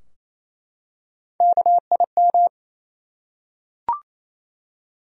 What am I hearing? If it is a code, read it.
Morse code, heard as a keyed tone, about 27 wpm, 700 Hz. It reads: KIM
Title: USA Common Names - Max 4 Letters - 28wpm